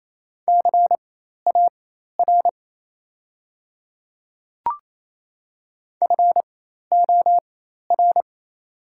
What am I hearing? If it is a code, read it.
Morse code, heard as a keyed tone, about 28 wpm, spaced as for 13 wpm, 700 Hz. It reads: CAR FOR
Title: 4x Character Spacing - Top 300 Words - 28wpm